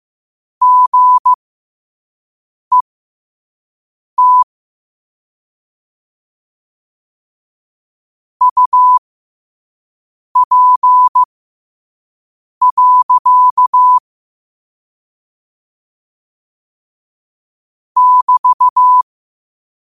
Morse code, heard as a keyed tone, about 15 wpm, spaced as for 5 wpm, 1000 Hz. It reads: GET UP. <BT>
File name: qotdfiles / 5wpm